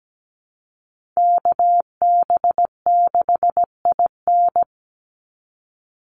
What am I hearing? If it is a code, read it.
Morse code, heard as a keyed tone, about 17 wpm, 700 Hz. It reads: KB6IN
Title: USA Call Signs 17wpm